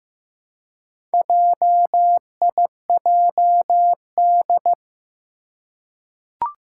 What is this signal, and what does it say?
Morse code, heard as a keyed tone, about 15 wpm, 700 Hz. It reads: JIJD
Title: Four Challenging Letter-Number Combinations 15wpm